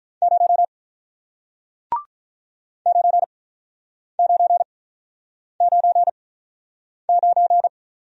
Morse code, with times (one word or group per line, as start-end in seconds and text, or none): 0.2-0.7 0
2.9-7.7 9999